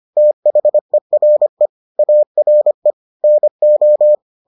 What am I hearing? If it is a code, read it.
Morse code, heard as a keyed tone, about 25 wpm, 600 Hz. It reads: THERE ARE NO